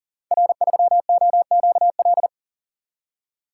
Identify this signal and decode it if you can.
Morse code, heard as a keyed tone, about 41 wpm, 700 Hz. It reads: R3OQL
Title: Code Groups - 5 Letter:Numbers - 40wpm